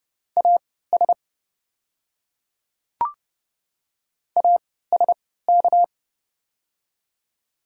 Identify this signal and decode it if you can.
Morse code, heard as a keyed tone, about 30 wpm, spaced as for 17 wpm, 690 Hz. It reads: AS ASK
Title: 3x Character Spacing - Top 300 Words - 30wpm